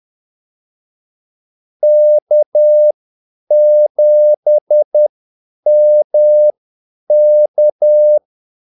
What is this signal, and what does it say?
Morse code, heard as a keyed tone, about 10 wpm, spaced as for 8 wpm, 600 Hz. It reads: K7MK
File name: Callsigns-10-8wpm